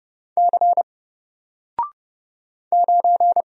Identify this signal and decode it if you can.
Morse code, heard as a keyed tone, about 29 wpm, 700 Hz. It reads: C 9
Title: Single Letter-Number Alt 30wpm